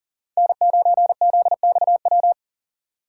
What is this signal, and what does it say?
Morse code, heard as a keyed tone, about 40 wpm, 700 Hz. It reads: N9ZXW
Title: USA Call Signs - 5 Characters - 3x Character Spacing - 40wpm